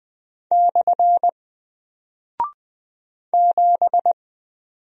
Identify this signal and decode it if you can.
Morse code, heard as a keyed tone, about 20 wpm, 700 Hz. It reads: / 7
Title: Single Character (Letter:Number:Punctuation:Prosign) Alt 20wpm